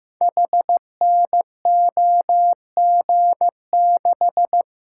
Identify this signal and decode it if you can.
Morse code, heard as a keyed tone, about 15 wpm, 700 Hz. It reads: HNOG6
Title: Code Groups - 5 Letter:Numbers - 15wpm